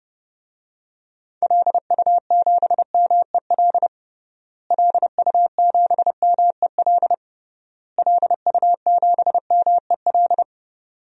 Morse code, heard as a keyed tone, about 29 wpm, 700 Hz. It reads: LU7MEL LU7MEL LU7MEL
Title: DXCC-100 Call Signs - 6 Characters - Repeated 3x - 30wpm